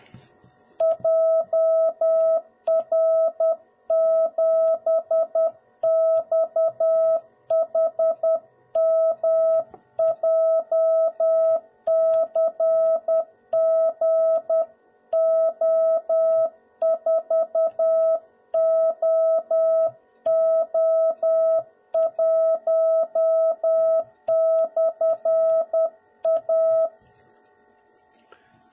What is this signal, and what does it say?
Morse code, heard as a keyed tone, about 10 wpm, 650 Hz. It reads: JR7XHM JCGO4OO1/A
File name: jh8cbh / 2021cw / cw01b